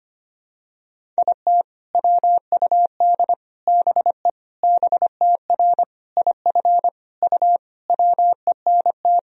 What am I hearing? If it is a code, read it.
Morse code, heard as a keyed tone, about 25 wpm, 700 Hz. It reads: IT WUD BE BTR IF U WENT
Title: Sentences from Top 400 Words and Abbreviations 25wpm